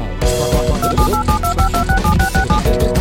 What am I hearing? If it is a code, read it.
DTMF, heard as telephone touch tones, tones of about 110 ms. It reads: 278725267667